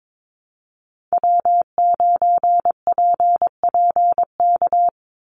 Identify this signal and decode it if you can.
Morse code, heard as a keyed tone, about 22 wpm, 700 Hz. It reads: W9PPK
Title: USA Call Signs 22wpm